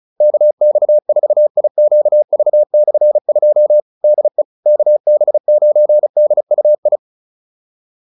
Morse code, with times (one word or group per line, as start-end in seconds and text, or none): 0.2-3.8 KX4IQV/2
4.0-4.4 DE
4.7-7.0 KB9DUI